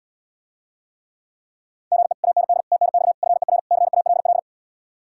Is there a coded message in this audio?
Yes: Morse code from 1.9 to 4.4 s, heard as a keyed tone, about 79 wpm, 700 Hz. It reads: GE AND TNX FER CALL